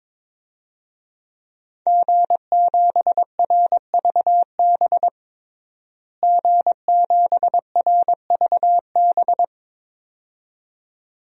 Morse code, heard as a keyed tone, about 22 wpm, 700 Hz. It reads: G7RVB G7RVB
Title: DXCC-100 Call Signs - 5 Characters - Repeated 2x - 22wpm